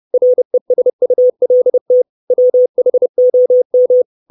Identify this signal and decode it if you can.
Morse code, heard as a keyed tone, about 30 wpm, 500 Hz. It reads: RESULT WHOM
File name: 632 Words 30WPM